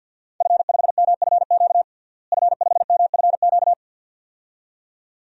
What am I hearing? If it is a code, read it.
Morse code, heard as a keyed tone, about 50 wpm, 700 Hz. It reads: F5MFQ F5MFQ
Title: DXCC-100 Call Signs - 5 Characters - Repeated 2x - 50wpm